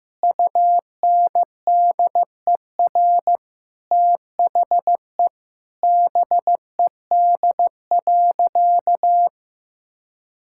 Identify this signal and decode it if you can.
Morse code, heard as a keyed tone, about 15 wpm, 700 Hz. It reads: UNDER THE BED.
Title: Sentences from Top 1000 Words - Review Part 6 - 15wpm